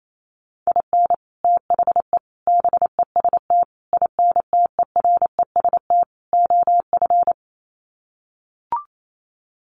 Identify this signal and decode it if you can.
Morse code, heard as a keyed tone, about 28 wpm, 700 Hz. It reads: IN THE BEST INTEREST OF
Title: Sets of 5 Words Encore 28wpm